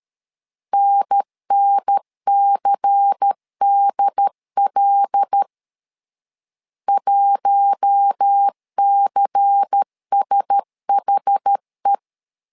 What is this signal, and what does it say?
Morse code, heard as a keyed tone, about 13 wpm, 800 Hz. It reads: NNCDL 1CSHE
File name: LEZIONI / 5 LBVC / 5 LBVC 12wpm 1955